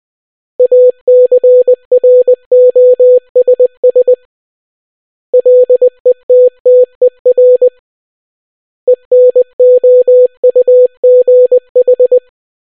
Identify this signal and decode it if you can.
Morse code, heard as a keyed tone, about 20 wpm, 500 Hz. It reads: ACROSS LETTER ENOUGH